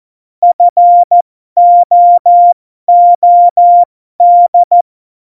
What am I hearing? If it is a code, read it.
Morse code, heard as a keyed tone, about 14 wpm, spaced as for 12 wpm, 700 Hz. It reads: FOOD
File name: Morsekurs / Witze